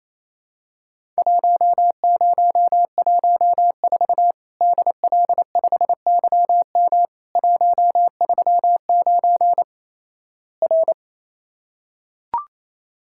Morse code, heard as a keyed tone, about 28 wpm, 700 Hz. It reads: 1014 DL5YM 139 R
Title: Worked All Europe DX Contest - QTC Exchange - 28wpm